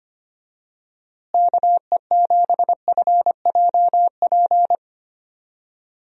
Morse code, heard as a keyed tone, about 25 wpm, 700 Hz. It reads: KE7FJP
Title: USA Call Signs - 6 Characters - Repeated 3x - 25wpm